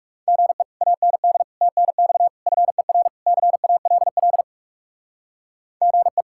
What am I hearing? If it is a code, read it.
Morse code, heard as a keyed tone, about 36 wpm, 700 Hz. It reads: GE AND TNX FER CALL GE
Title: Speed-Racing Common QSO Phrases Encore - 30wpm